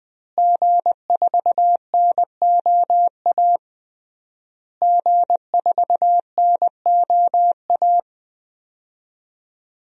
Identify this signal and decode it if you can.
Morse code, heard as a keyed tone, about 20 wpm, 700 Hz. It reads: G4NOA G4NOA
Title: DXCC-100 Call Signs - 5 Characters - Repeated 3x - 20wpm